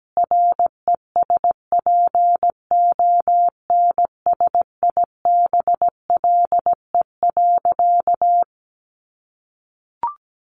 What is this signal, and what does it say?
Morse code, heard as a keyed tone, about 17 wpm, 700 Hz. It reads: RESPONSIBLE.
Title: A Sound Of Thunder 17wpm